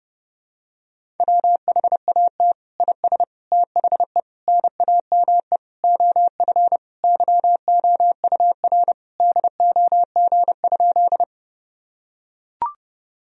Morse code, heard as a keyed tone, about 29 wpm, 700 Hz. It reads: WHAT IS THE NAME OF YOUR DOG?
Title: Sentences from Top 1000 Words - Review Part 2 - 30wpm